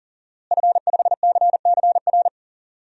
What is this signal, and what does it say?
Morse code, heard as a keyed tone, about 37 wpm, 700 Hz. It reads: F5CCR
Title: DXCC-100 Call Signs - 5 Characters - Repeated 3x - 40wpm